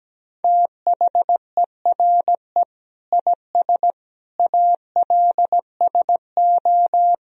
Morse code, heard as a keyed tone, about 17 wpm, 700 Hz. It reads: THERE IS ALSO
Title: Sentences from Top 1000 Words - Review Part 1 - 17wpm